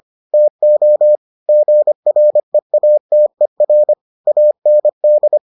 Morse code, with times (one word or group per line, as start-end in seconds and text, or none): 0.3-1.2 TO
1.5-3.9 GREATER
4.3-5.4 AND